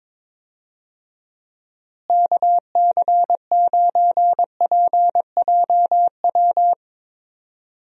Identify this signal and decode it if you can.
Morse code, heard as a keyed tone, about 22 wpm, 700 Hz. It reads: KC9PJW
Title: USA Call Signs - 6 Characters - Repeated 3x - 22wpm